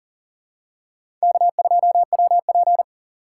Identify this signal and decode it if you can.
Morse code, heard as a keyed tone, about 38 wpm, 690 Hz. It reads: K2WP
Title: USA Call Signs - 4 Characters - 2x Character Spacing - 40wpm